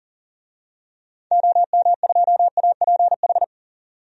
Morse code, heard as a keyed tone, about 40 wpm, 700 Hz. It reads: OM2APH
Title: DXCC-100 Call Signs - 6 Characters - 40wpm